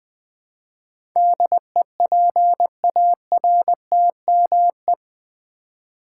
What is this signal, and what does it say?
Morse code, heard as a keyed tone, about 20 wpm, 700 Hz. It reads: DEPARTME
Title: Build A Word - 4 Letters At A Time - 10-Letter Words - 20wpm